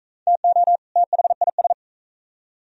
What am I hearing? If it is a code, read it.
Morse code, heard as a keyed tone, about 44 wpm, 700 Hz. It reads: TO THIS